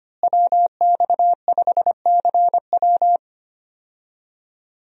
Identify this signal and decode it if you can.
Morse code, heard as a keyed tone, about 25 wpm, 700 Hz. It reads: WX5CW
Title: USA Call Signs - 5 Characters - Repeated 2x - 25wpm